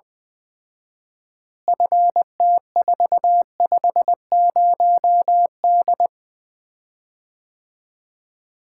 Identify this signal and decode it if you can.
Morse code, heard as a keyed tone, about 20 wpm, 700 Hz. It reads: FT450D
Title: Common QSO Phrases - 4x Word-Spacing - 20wpm